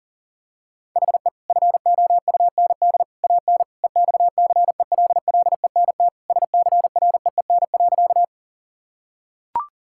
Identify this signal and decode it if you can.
Morse code, heard as a keyed tone, about 37 wpm, 700 Hz. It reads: HE FOUND AN EXCELLENT SCREEN.
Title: Sentences from Top 1000 Words - Review Part 6 - 40wpm